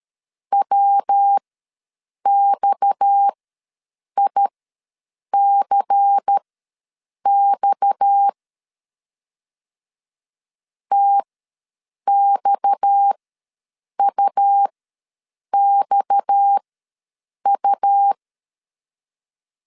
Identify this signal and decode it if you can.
Morse code, heard as a keyed tone, about 13 wpm, spaced as for 7 wpm, 800 Hz. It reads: WXICX TXUXU
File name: LEZIONI / 6 PKX0 / 6 PKX ws 12wpm 3106